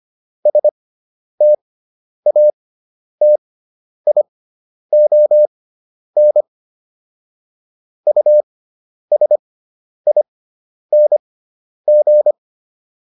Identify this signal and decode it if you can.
Morse code, heard as a keyed tone, about 25 wpm, spaced as for 10 wpm, 600 Hz. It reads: STATION USING